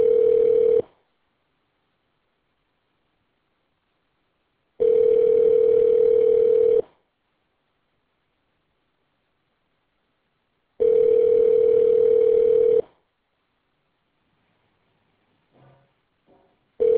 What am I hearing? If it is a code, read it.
Ringback tone (North America), 440 + 480 Hz.